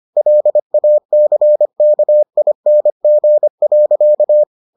Morse code, heard as a keyed tone, about 25 wpm, 600 Hz. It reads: LACKING.